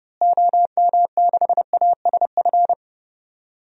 Morse code, heard as a keyed tone, about 30 wpm, 700 Hz. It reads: OM6ASF
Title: DXCC-100 Call Signs - 6 Characters - Repeated 3x - 30wpm